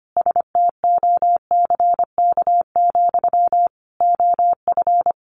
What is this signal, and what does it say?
Morse code, heard as a keyed tone, about 25 wpm, 700 Hz. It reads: STOCK, OF
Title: Sentences from Top 800 Words 25wpm